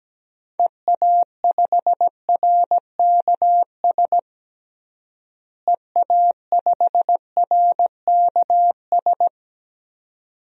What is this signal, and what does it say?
Morse code, heard as a keyed tone, about 17 wpm, 700 Hz. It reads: EA5RKS EA5RKS
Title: DXCC-100 Call Signs - 6 Characters - Repeated 2x - 17wpm